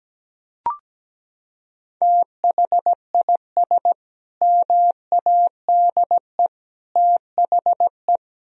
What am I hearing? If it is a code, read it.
Morse code, heard as a keyed tone, about 17 wpm, 700 Hz. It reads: THIS MADE THE
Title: Sentences from Top 1000 Words - Review Part 2 - 17wpm